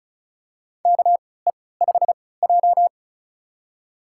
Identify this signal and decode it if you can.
Morse code, heard as a keyed tone, about 36 wpm, spaced as for 20 wpm, 700 Hz. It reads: KE5J
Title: USA Call Signs - 4 Characters - 3x Character Spacing - 35wpm